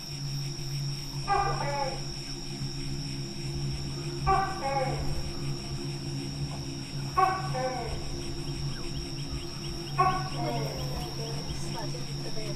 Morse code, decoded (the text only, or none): none